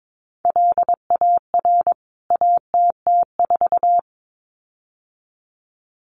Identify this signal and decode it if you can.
Morse code, heard as a keyed tone, about 22 wpm, 700 Hz. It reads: LAR ATT4
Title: CWT Exchange 22wpm